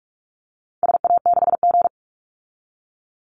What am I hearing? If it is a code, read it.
Morse code, heard as a keyed tone, about 44 wpm, 700 Hz. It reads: SA6G
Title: Call Signs - 45wpm